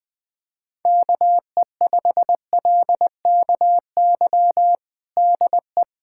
Morse code, heard as a keyed tone, about 21 wpm, 700 Hz. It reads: KE5LKY DE
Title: USA Callsign DE Callsign - 6 Characters - 20wpm